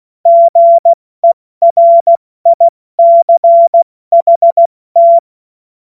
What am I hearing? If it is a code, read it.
Morse code, heard as a keyed tone, about 16 wpm, spaced as for 14 wpm, 680 Hz. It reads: GERICHT
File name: Morsekurs / Witze / Witz 20. September 2022 70 F80